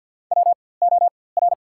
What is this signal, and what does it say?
Morse code, heard as a keyed tone, about 54 wpm, spaced as for 24 wpm, 700 Hz. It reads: WOR
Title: 4x Character Spacing - Top 500 Words - 50wpm